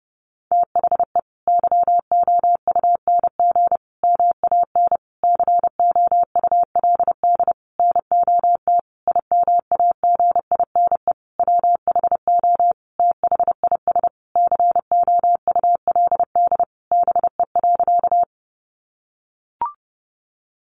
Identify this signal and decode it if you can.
Morse code, heard as a keyed tone, about 30 wpm, 700 Hz. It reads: THE YOUNG MAN COULD NOT IMAGINE WHO THIS COULD BE.
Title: Sentences from Top 1400 Words 30wpm